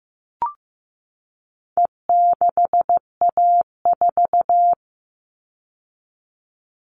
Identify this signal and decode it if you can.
Morse code, heard as a keyed tone, about 15 wpm, 700 Hz. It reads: E6A4
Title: All Cut Numbers - 4-Digit 15wpm